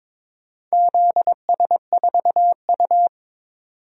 Morse code, heard as a keyed tone, about 22 wpm, 700 Hz. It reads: ZS4U
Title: DXCC-100 Call Signs - 4 Characters - 22wpm